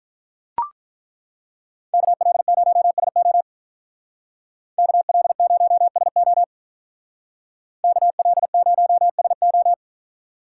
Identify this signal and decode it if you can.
Morse code, heard as a keyed tone, about 43 wpm, 700 Hz. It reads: KL0SO KL0SO KL0SO
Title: Speed Racing - USA 2x2 Call Signs 35wpm